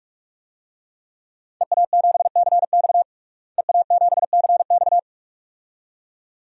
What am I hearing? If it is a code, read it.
Morse code, heard as a keyed tone, about 46 wpm, 700 Hz. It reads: EA7CX EA7CX
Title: DXCC-100 Call Signs - 5 Characters - Repeated 2x - 45wpm